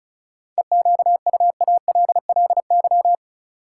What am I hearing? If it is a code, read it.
Morse code, heard as a keyed tone, about 34 wpm, 700 Hz. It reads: EQUALLY
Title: Sentences from Top 1400 Words - New Word Drill - 35wpm